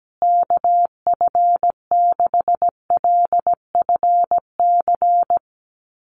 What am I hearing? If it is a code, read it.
Morse code, heard as a keyed tone, about 17 wpm, 700 Hz. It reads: KF6LFC
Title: USA Call Signs Repeated 3x 17wpm